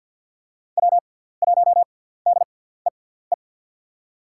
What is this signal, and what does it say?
Morse code, heard as a keyed tone, about 46 wpm, spaced as for 17 wpm, 700 Hz. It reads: W1DEE
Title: USA Call Signs - 5 Characters - 6x Character Spacing - 50wpm